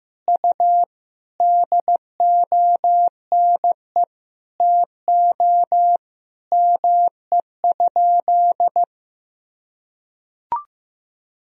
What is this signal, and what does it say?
Morse code, heard as a keyed tone, about 15 wpm, 700 Hz. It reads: U DONE TO ME?
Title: Sentences from Top 500 Words and Abbreviations 15wpm